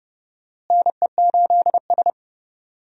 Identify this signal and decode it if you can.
Morse code, heard as a keyed tone, about 32 wpm, 700 Hz. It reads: NE8S
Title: USA Call Signs - 4 Characters - Repeated 3x - 30wpm